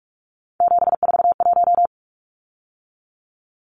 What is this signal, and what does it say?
Morse code, heard as a keyed tone, about 44 wpm, 700 Hz. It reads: 741
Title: Numbers - 3-Digit 45wpm